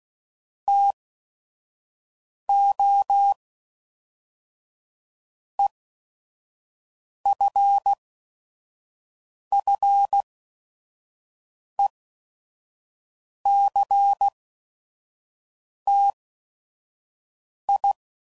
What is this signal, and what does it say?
Morse code, heard as a keyed tone, about 16 wpm, 800 Hz. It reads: T O E F F E C T I